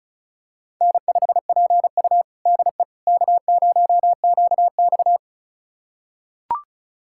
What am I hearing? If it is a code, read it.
Morse code, heard as a keyed tone, about 36 wpm, 700 Hz. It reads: N5PU DE K0QX
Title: USA Callsign DE Callsign - 4 Characters - 35wpm